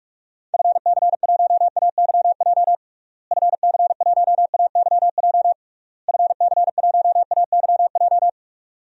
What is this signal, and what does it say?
Morse code, heard as a keyed tone, about 41 wpm, 700 Hz. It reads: FC1AYJ FC1AYJ FC1AYJ
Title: DXCC-100 Call Signs - 6 Characters - Repeated 3x - 45wpm